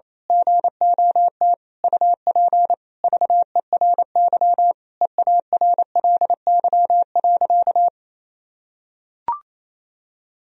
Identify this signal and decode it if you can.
Morse code, heard as a keyed tone, about 28 wpm, 700 Hz. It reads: GOT UP VERY EARLY.
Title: Sentences from Top 1000 Words - Review Part 2 - 28wpm